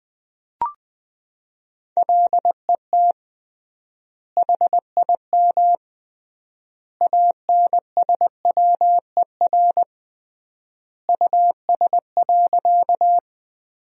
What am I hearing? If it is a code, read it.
Morse code, heard as a keyed tone, about 20 wpm, 700 Hz. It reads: LET HIM ANSWER US.
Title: 3x Word-Spacing - Sentences from Top 300 Words - 20wpm